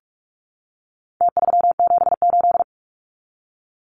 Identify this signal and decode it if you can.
Morse code, heard as a keyed tone, about 44 wpm, 700 Hz. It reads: T378
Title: Numbers - Contest Style - 4-Digit 45wpm